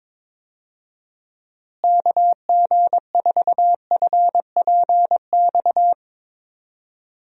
Morse code, heard as a keyed tone, about 22 wpm, 700 Hz. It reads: KG4FPX
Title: USA Call Signs - 6 Characters - Repeated 3x - 22wpm